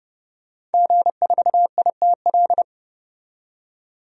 Morse code, heard as a keyed tone, about 30 wpm, 700 Hz. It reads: G4ITL
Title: DXCC-100 Call Signs - 5 Characters - 30wpm